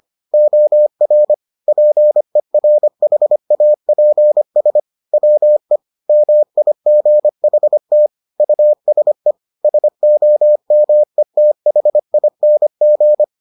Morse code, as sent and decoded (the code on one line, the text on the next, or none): --- .-. / .--. .-. .... .- .--. ... / .-- . / -- .. --. .... - / ..- ... / ... --- -- . - .... .. -. --.
OR PERHAPS WE MIGHT USE SOMETHING